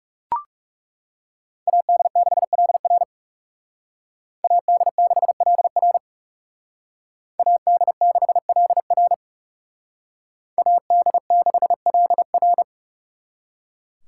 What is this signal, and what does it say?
Morse code, heard as a keyed tone, about 38 wpm, 700 Hz. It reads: AD6LR AD6LR AD6LR AD6LR